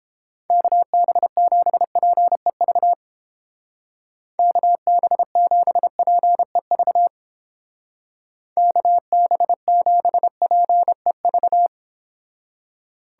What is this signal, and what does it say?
Morse code, heard as a keyed tone, about 30 wpm, 700 Hz. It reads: KB7PEV KB7PEV KB7PEV